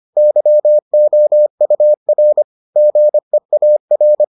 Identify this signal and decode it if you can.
Morse code, heard as a keyed tone, about 25 wpm, 600 Hz. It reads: YOUR GEAR